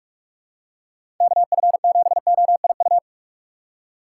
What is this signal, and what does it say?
Morse code, heard as a keyed tone, about 44 wpm, 700 Hz. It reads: KF7OIU
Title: USA Call Signs - 6 Characters - 6x Character Spacing - 45wpm